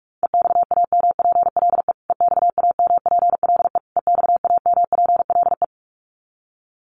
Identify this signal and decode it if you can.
Morse code, heard as a keyed tone, about 44 wpm, 700 Hz. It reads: EXAMPLE EXAMPLE EXAMPLE
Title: Top 500 Words Repeated 45wpm